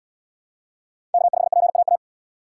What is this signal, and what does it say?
Morse code, heard as a keyed tone, about 77 wpm, 700 Hz. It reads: K5PAA